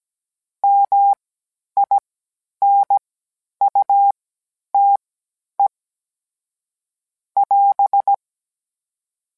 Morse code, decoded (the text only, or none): MINUTE <AS>